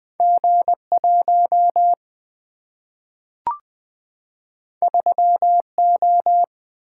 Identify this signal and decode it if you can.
Morse code, heard as a keyed tone, about 20 wpm, 700 Hz. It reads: G 1 3 O
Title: Single-Letter Single-Number 20wpm